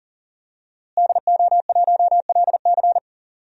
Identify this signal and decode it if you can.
Morse code, heard as a keyed tone, about 37 wpm, 700 Hz. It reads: DO1LC
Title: DXCC-100 Call Signs - 5 Characters - Repeated 2x - 40wpm